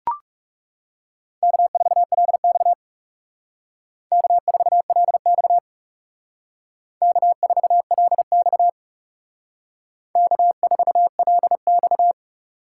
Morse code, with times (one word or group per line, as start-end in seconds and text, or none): 1.4-2.7 K4LX
4.1-5.6 K4LX
7.0-8.7 K4LX
10.2-12.1 K4LX